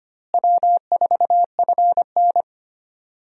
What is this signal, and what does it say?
Morse code, heard as a keyed tone, about 25 wpm, 700 Hz. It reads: W4FN
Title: USA Call Signs - 4 Characters - Repeated 2x - 25wpm